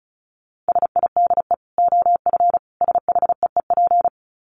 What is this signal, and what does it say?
Morse code, heard as a keyed tone, about 34 wpm, 700 Hz. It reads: SIDE OF SHEEP
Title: Dit Challenge 35wpm